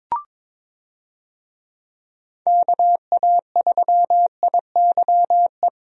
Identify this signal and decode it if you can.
Morse code, heard as a keyed tone, about 22 wpm, 700 Hz. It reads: KA3IYE
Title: USA Call Signs - 6 Characters - Repeated 2x - 22wpm